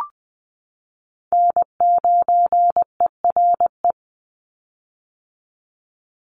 Morse code, heard as a keyed tone, about 20 wpm, 700 Hz. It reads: N9ERE
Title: USA Call Signs 20wpm